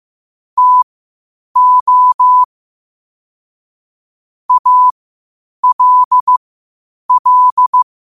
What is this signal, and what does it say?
Morse code, heard as a keyed tone, about 15 wpm, spaced as for 8 wpm, 1000 Hz. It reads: TO ALL
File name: qotdfiles / 8wpm